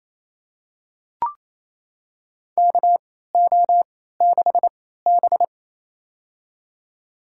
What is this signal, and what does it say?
Morse code, heard as a keyed tone, about 27 wpm, spaced as for 16 wpm, 700 Hz. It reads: KO6B
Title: USA Call Signs - 3x Character Spacing - 28wpm